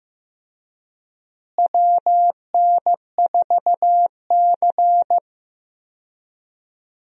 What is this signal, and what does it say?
Morse code, heard as a keyed tone, about 15 wpm, 700 Hz. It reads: WN4C
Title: USA Call Signs - 4 Characters - Repeated 3x - 15wpm